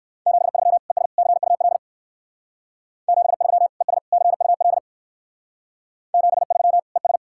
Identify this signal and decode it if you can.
Morse code, heard as a keyed tone, about 58 wpm, 700 Hz. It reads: 73 ES CUL 73 ES CUL 73 ES